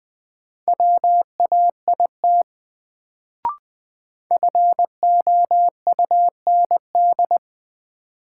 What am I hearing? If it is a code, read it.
Morse code, heard as a keyed tone, about 20 wpm, 700 Hz. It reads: WAIT FOUND
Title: Sets of 5 Words Encore - New Word Drill - 20wpm